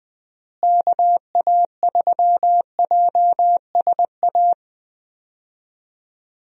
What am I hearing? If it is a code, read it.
Morse code, heard as a keyed tone, about 21 wpm, 700 Hz. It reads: KA3JSA
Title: USA Call Signs - 6 Characters - Repeated 3x - 20wpm